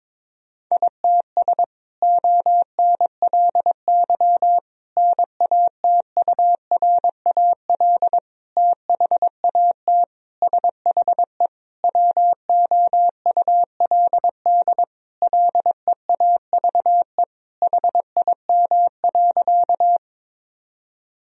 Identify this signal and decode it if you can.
Morse code, heard as a keyed tone, about 22 wpm, 700 Hz. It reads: ITS ONLY NATURAL THAT SHE WOULD LEAVE HIM.
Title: Sentences from Top 1000 Words - Review Part 4 - 22wpm